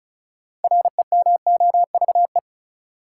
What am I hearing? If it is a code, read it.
Morse code, heard as a keyed tone, about 34 wpm, 700 Hz. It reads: REMOVE
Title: Sentences from Top 800 Words - New Word Drill - 35wpm